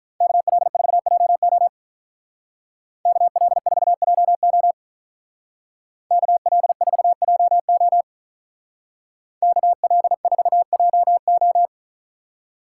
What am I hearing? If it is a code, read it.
Morse code, heard as a keyed tone, about 41 wpm, 700 Hz. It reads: KL4JO KL4JO KL4JO KL4JO
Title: Speed Racing - USA 2x2 Call Signs 35wpm